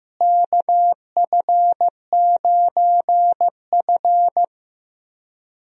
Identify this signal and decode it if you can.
Morse code, heard as a keyed tone, about 15 wpm, 700 Hz. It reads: KF9F
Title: USA Call Signs - 4 Characters - Repeated 2x - 15wpm